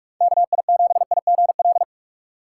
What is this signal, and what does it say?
Morse code, heard as a keyed tone, about 46 wpm, 700 Hz. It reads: KI7IGL
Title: USA Call Signs - 6x Character Spacing - 45wpm